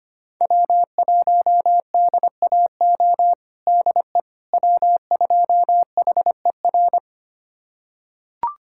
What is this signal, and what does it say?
Morse code, heard as a keyed tone, about 25 wpm, 700 Hz. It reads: W1DAO DE W2HER
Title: USA Callsign DE Callsign - 5 Characters - 25wpm